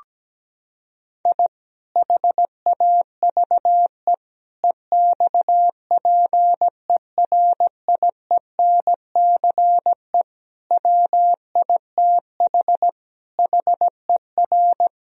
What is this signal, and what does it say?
Morse code, heard as a keyed tone, about 17 wpm, 700 Hz. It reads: I HAVE EXPERIENCE WITH HER
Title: Sentences from Top 1000 Words - Review Part 3 - 17wpm